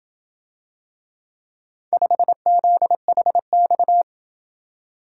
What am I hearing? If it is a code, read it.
Morse code, heard as a keyed tone, about 28 wpm, 700 Hz. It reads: 5ZHX